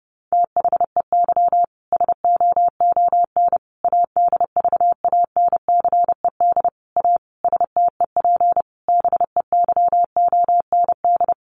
Morse code, heard as a keyed tone, about 30 wpm, 700 Hz. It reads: THEY SOON ADVANCED A STEP BEYOND
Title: Sentences from Top 1200 Words 30wpm